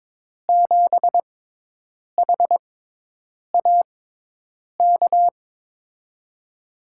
Morse code, heard as a keyed tone, about 22 wpm, spaced as for 8 wpm, 700 Hz. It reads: 7HAK
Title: USA Call Signs - 6 Characters - 6x Character Spacing - 22wpm